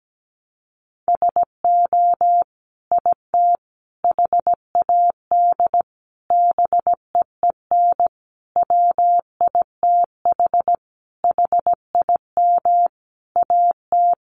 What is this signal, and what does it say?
Morse code, heard as a keyed tone, about 17 wpm, 700 Hz. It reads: SO IT HAD BEEN WITH HIM AT
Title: Sentences from Top 1200 Words 17wpm